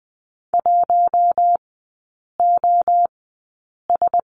Morse code, heard as a keyed tone, about 20 wpm, spaced as for 8 wpm, 700 Hz. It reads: 1OS
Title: 041 Lesson 8e - Number 1 - Three Characters 20wpm